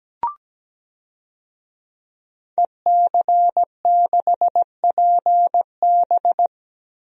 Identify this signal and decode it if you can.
Morse code, heard as a keyed tone, about 17 wpm, 700 Hz. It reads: EC6PB
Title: DXCC-100 Call Signs - 5 Characters - 17wpm